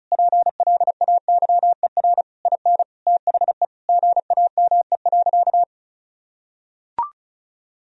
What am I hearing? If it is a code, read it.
Morse code, heard as a keyed tone, about 36 wpm, 700 Hz. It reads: PLAYER IN THE GAME.
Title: Sentences from Top 1000 Words - Review Part 4 - 35wpm